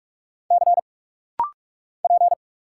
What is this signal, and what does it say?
Morse code, heard as a keyed tone, about 46 wpm, 700 Hz. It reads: C P
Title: Single Letter-Number Alt - Rapid-Fire 45wpm